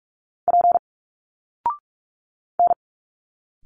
Morse code, heard as a keyed tone, about 46 wpm, 700 Hz. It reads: P N
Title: Single Letters 45wpm